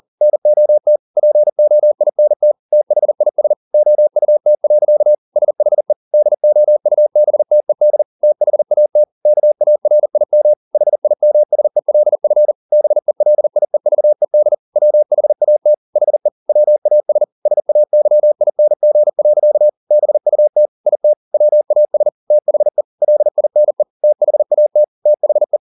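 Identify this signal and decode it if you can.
Morse code, heard as a keyed tone, about 40 wpm, 600 Hz. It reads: NOT POINT THIS OUT. SHE DOUBTED THAT KARIM HIMSELF BELIEVED WHAT HE WAS SAYING. BUT IT WAS THE LINE THAT THE
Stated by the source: Morse code at 40 wpm, practice text, CW med Gnister.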